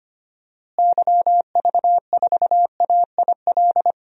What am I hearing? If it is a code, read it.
Morse code, heard as a keyed tone, about 25 wpm, 700 Hz. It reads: YV4AIL
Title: DXCC-100 Call Signs - 6 Characters - Repeated 2x - 25wpm